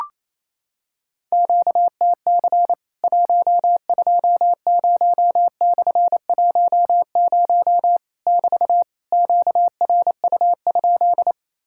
Morse code, heard as a keyed tone, about 28 wpm, 700 Hz. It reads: QTC 120/10 <BT> QRU?
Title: Worked All Europe DX Contest - QTC Exchange - 28wpm